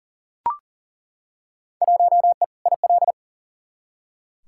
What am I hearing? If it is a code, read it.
Morse code, heard as a keyed tone, about 40 wpm, 700 Hz. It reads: 1E IL
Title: Field Day Exchanges - 40wpm